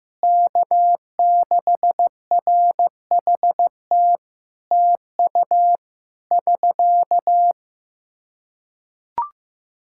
Morse code, heard as a keyed tone, about 15 wpm, 700 Hz. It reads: K6RHT TU <SK>
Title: K1USN SST QSO Practice 15wpm